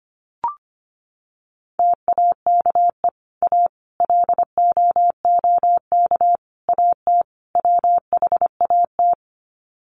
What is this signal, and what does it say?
Morse code, heard as a keyed tone, about 25 wpm, 700 Hz. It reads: TAKE A LOOK AT WHAT
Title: Sets of 5 Words Encore2 25wpm